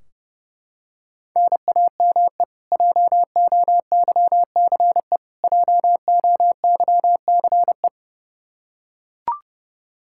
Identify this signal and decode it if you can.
Morse code, heard as a keyed tone, about 30 wpm, 700 Hz. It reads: NAME JOYCE JOYCE
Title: Practice QSO - Exchange 1 - Beginning - 30wpm